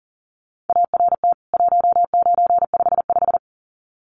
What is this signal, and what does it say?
Morse code, heard as a keyed tone, about 40 wpm, 700 Hz. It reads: ART 1955
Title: CWT Exchange 40wpm